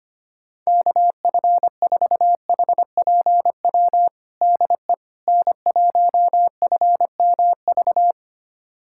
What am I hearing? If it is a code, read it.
Morse code, heard as a keyed tone, about 25 wpm, 700 Hz. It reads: KF4HPW DE N1FMV
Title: USA Callsign DE Callsign 25wpm